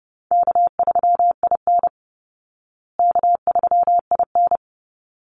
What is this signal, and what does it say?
Morse code, heard as a keyed tone, about 30 wpm, 700 Hz. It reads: K3IN K3IN
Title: Easy Call Signs Repeated 2x - 30wpm